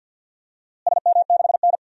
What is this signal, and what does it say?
Morse code, heard as a keyed tone, about 52 wpm, 700 Hz. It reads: SM6N
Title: DXCC-100 Call Signs - 4 Characters - Repeated 2x - 50wpm